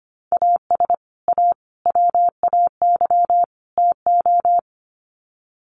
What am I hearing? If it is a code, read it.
Morse code, heard as a keyed tone, about 25 wpm, 700 Hz. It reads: AS A WAY TO
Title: Sets of 4 Words 25wpm